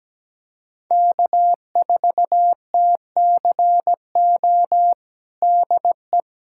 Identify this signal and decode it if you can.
Morse code, heard as a keyed tone, about 17 wpm, 700 Hz. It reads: K4TCO DE
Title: USA Callsign DE Callsign - 5 Characters - 17wpm